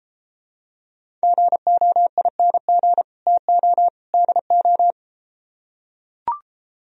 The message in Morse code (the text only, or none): GOING TO DO